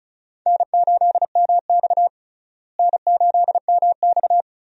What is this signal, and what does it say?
Morse code, heard as a keyed tone, about 36 wpm, 700 Hz. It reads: N8MX N8MX
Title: USA Call Signs - 4 Characters - Repeated 2x - 35wpm